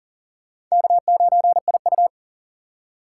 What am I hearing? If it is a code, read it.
Morse code, heard as a keyed tone, about 39 wpm, 700 Hz. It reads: K9IU
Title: USA Call Signs - 4 Characters - 6x Character Spacing - 40wpm